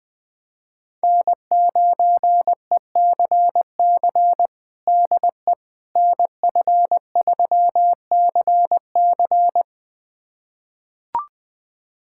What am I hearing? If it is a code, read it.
Morse code, heard as a keyed tone, about 20 wpm, 700 Hz. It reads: N9ECC DE NF3CC
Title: USA Callsign DE Callsign 20wpm